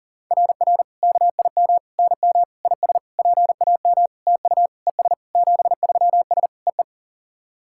Morse code, heard as a keyed tone, about 42 wpm, 700 Hz. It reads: RR KIM NM IS PAM TU ES 73S EE
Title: SOTA QSO Practice 40wpm